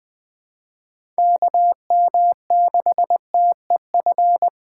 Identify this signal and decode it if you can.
Morse code, heard as a keyed tone, about 20 wpm, 700 Hz. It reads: KM6TEF
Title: USA Call Signs - 6 Characters - 2x Character Spacing - 20wpm